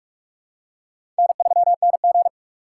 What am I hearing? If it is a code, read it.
Morse code, heard as a keyed tone, about 46 wpm, 700 Hz. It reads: N3NG
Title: USA Call Signs - 2x Character Spacing - 45wpm